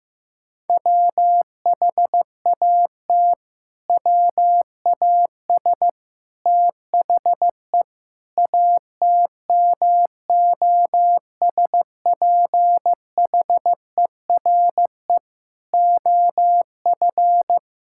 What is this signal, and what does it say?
Morse code, heard as a keyed tone, about 15 wpm, 700 Hz. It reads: WHAT WAS THE ATMOSPHERE OF
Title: Sentences from Top 1600 Words 15wpm